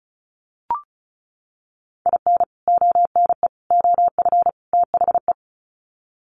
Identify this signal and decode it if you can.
Morse code, heard as a keyed tone, about 34 wpm, 700 Hz. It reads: IN ONE OF THE
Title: Sets of 4 Words 35wpm